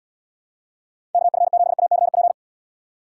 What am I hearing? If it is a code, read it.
Morse code, heard as a keyed tone, about 78 wpm, 700 Hz. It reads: KF7ACP